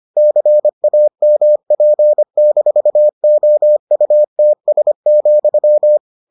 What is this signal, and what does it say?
Morse code, heard as a keyed tone, about 25 wpm, 600 Hz. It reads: CAMP-OUTS,